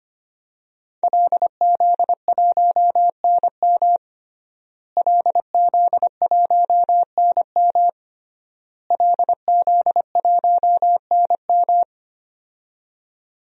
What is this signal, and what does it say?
Morse code, heard as a keyed tone, about 25 wpm, 700 Hz. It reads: LZ1NM LZ1NM LZ1NM
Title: DXCC-100 Call Signs - 5 Characters - Repeated 3x - 25wpm